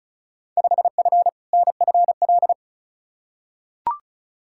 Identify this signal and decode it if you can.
Morse code, heard as a keyed tone, about 34 wpm, 700 Hz. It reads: 5F NFL
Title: Field Day Exchanges Encore - 35wpm